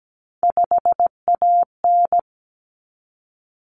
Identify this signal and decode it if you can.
Morse code, heard as a keyed tone, about 17 wpm, 700 Hz. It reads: 5AN
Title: Common Cut Numbers - 3-Digit 17wpm